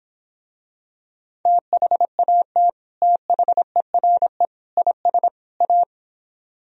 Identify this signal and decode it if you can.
Morse code, heard as a keyed tone, about 26 wpm, 700 Hz. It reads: THAT THERE IS A